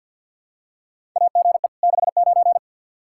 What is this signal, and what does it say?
Morse code, heard as a keyed tone, about 48 wpm, 700 Hz. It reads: AGE 69
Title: Practice QSO - Exchange 3 - More Info - 50wpm